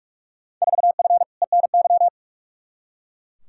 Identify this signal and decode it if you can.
Morse code, heard as a keyed tone, about 41 wpm, 700 Hz. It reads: 4F ENY
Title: Field Day Exchanges - 45wpm